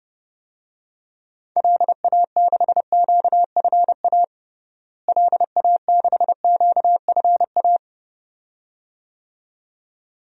Morse code, heard as a keyed tone, about 30 wpm, 700 Hz. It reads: LA6QFA LA6QFA
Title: DXCC-100 Call Signs - 6 Characters - Repeated 2x - 30wpm